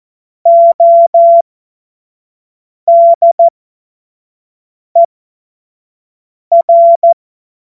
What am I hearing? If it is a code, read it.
Morse code, heard as a keyed tone, about 14 wpm, spaced as for 5 wpm, 680 Hz. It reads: ODER
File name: Morsekurs / Witze / Witz 05. Oktober 2021, 25 F70